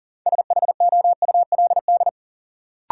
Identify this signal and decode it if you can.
Morse code, heard as a keyed tone, about 37 wpm, 700 Hz. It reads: SHOULD
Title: Sentences from Top 300 Words - New Word Drill - 40wpm